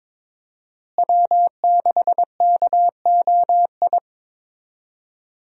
Morse code, heard as a keyed tone, about 21 wpm, 700 Hz. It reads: W6KOI
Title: USA Call Signs - 5 Characters - Repeated 3x - 22wpm